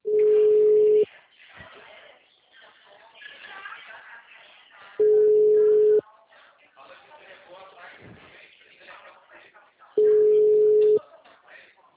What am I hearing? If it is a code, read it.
Ringback tone (Europe, 425 Hz), with speech.
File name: out- -2008- - - .4530